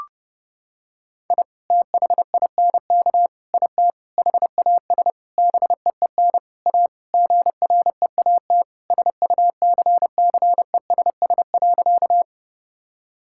Morse code, heard as a keyed tone, about 30 wpm, 700 Hz. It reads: I THINK IT HAS BEEN A GREAT SUCCESS.
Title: Sentences from Top 1000 Words - Review Part 6 - 30wpm